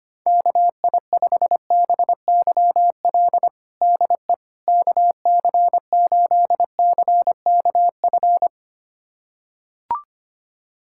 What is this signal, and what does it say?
Morse code, heard as a keyed tone, about 25 wpm, 700 Hz. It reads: KI5BYL DE KC8CKF
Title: USA Callsign DE Callsign - 6 Characters - 25wpm